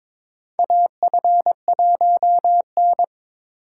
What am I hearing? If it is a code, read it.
Morse code, heard as a keyed tone, about 21 wpm, 700 Hz. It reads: AF1N
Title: USA Call Signs - 4 Characters - 2x Character Spacing - 22wpm